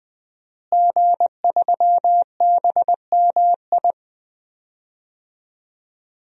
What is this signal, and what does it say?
Morse code, heard as a keyed tone, about 20 wpm, 700 Hz. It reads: G3BMI
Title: DXCC-100 Call Signs - 5 Characters - Repeated 2x - 20wpm